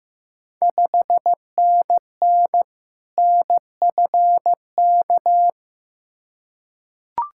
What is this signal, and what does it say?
Morse code, heard as a keyed tone, about 15 wpm, 690 Hz. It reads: 5NN NFK
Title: Ontario QSO Party Exchange 15wpm